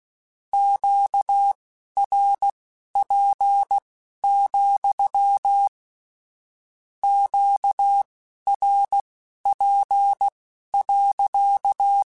Morse code, heard as a keyed tone, about 16 wpm, spaced as for 12 wpm, 800 Hz. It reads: QRP, QRP.
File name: Q Calls / Q Calls-mp3s / Q Calls sppc @15